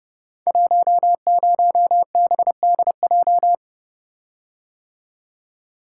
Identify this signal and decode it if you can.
Morse code, heard as a keyed tone, about 30 wpm, 700 Hz. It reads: 10BDJ
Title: Code Groups - 5 Letter:Numbers - 30wpm